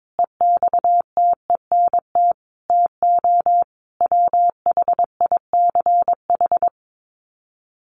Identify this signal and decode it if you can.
Morse code, heard as a keyed tone, about 22 wpm, 700 Hz. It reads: EXTENT TO WHICH
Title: Sets of 4 Words 22wpm